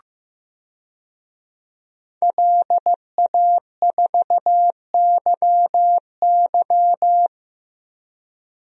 Morse code, heard as a keyed tone, about 15 wpm, 700 Hz. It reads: LA4YY
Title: DXCC-100 Call Signs - 5 Characters - Repeated 2x - 15wpm